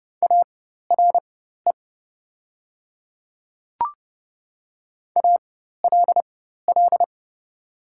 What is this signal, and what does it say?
Morse code, heard as a keyed tone, about 30 wpm, spaced as for 14 wpm, 700 Hz. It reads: ARE ALL
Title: 4x Character Spacing - Top 100 Words - 30wpm